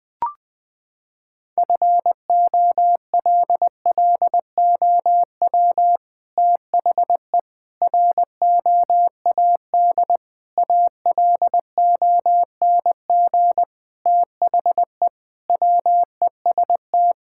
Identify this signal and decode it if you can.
Morse code, heard as a keyed tone, about 20 wpm, 700 Hz. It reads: FOLLOW THE ROAD ALONG THE WEST
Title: Sentences from Top 1000 Words - Review Part 6 - 20wpm